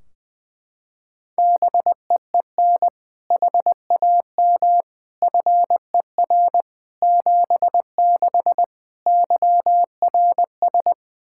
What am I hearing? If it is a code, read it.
Morse code, heard as a keyed tone, about 20 wpm, 700 Hz. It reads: BEEN HAM FER 76 YRS
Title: Practice QSO - Exchange 3 - More Info - 20wpm